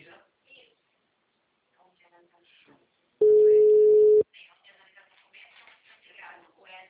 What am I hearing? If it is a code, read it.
Ringback tone (Europe, 425 Hz), with speech.